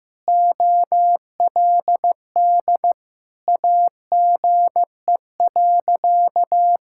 Morse code, heard as a keyed tone, about 15 wpm, 700 Hz. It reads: OLD AGE.
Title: Sentences from Top 1000 Words - Review Part 6 - 15wpm